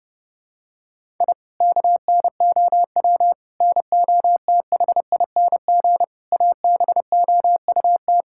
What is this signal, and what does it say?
Morse code, heard as a keyed tone, about 32 wpm, 700 Hz. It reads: I KNOW NOTHING ABOUT
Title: Sentences from Top 1000 Words - Review Part 6 - 30wpm